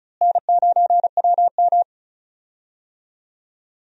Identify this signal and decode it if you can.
Morse code, heard as a keyed tone, about 36 wpm, 700 Hz. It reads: N9WM
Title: USA Call Signs - 4 Characters - Repeated 2x - 35wpm